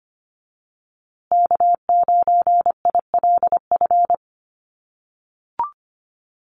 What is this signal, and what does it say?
Morse code, heard as a keyed tone, about 25 wpm, 700 Hz. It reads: K9ILF
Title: USA Call Signs Repeated 3x 25wpm